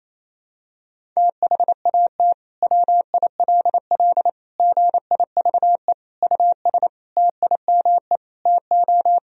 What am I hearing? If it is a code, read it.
Morse code, heard as a keyed tone, about 28 wpm, 700 Hz. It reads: THAT WILL GIVE US TIME TO
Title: Sentences from Top 1000 Words - Review Part 1 - 28wpm